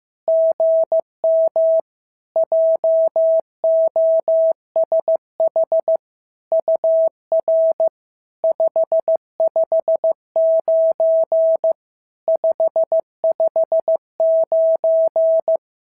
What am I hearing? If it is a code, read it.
Morse code, heard as a keyed tone, about 15 wpm, 650 Hz. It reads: GM JOSH UR 559 559
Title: SOTA QSO Practice 15wpm